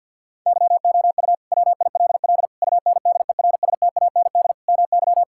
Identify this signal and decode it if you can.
Morse code, heard as a keyed tone, about 50 wpm, 700 Hz. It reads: YOU WILL UNDERSTAND MY
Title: Sentences from Top 1600 Words 50wpm